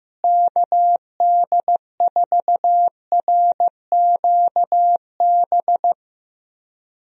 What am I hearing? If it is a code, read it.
Morse code, heard as a keyed tone, about 15 wpm, 700 Hz. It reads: KD4RQB
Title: USA Call Signs - 6 Characters - Repeated 3x - 15wpm